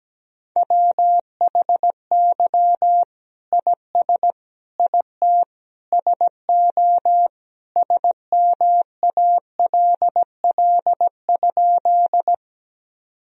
Morse code, heard as a keyed tone, about 17 wpm, 700 Hz. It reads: WHY IS IT SO SMALL?
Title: Sentences from Top 1000 Words - Review Part 1 - 17wpm